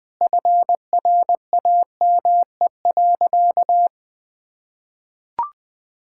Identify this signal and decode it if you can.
Morse code, heard as a keyed tone, about 20 wpm, 700 Hz. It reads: FRAME.
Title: Sentences from Top 1000 Words - Review Part 6 - 20wpm